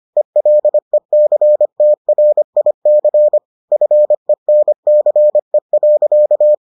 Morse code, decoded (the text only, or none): ELECTRIC FENCE.